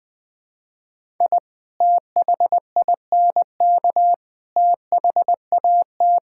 Morse code, heard as a keyed tone, about 21 wpm, 700 Hz. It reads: I THINK THAT
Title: Build A Phrase - 4-Word Phrases - 20wpm